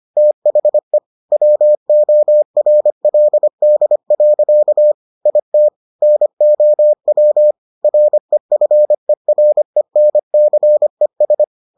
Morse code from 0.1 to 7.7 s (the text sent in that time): THE WORLD. IT NOW